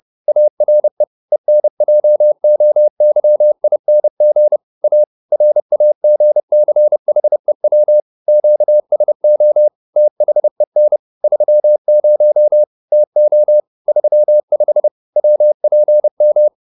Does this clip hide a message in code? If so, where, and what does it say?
Yes: Morse code from 0.3 to 16.5 s, heard as a keyed tone, about 30 wpm, 600 Hz. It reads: ARE ENJOYING A RAGCHEW QSO THEN 30 TO 35 WPM